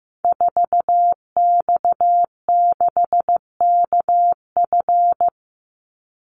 Morse code, heard as a keyed tone, about 15 wpm, 700 Hz. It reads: 4X6KF
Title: Call Signs Repeated 3x - 15wpm